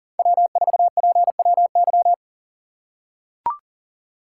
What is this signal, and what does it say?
Morse code, heard as a keyed tone, about 40 wpm, 700 Hz. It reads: W4PWY
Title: USA Call Signs - 5 Characters - 6x Character Spacing - 40wpm